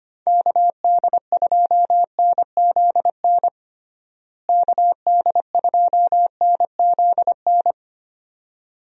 Morse code, heard as a keyed tone, about 25 wpm, 700 Hz. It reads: KD2NZN KD2NZN
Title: USA Call Signs - 6 Characters - Repeated 2x - 25wpm